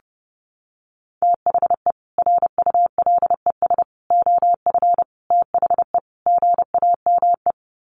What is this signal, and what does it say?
Morse code, heard as a keyed tone, about 30 wpm, 700 Hz. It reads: THE RULES OF THE GAME
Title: Sets of 5 Words Encore2 30wpm